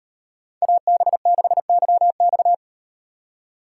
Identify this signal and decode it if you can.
Morse code, heard as a keyed tone, about 38 wpm, 700 Hz. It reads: AB6YX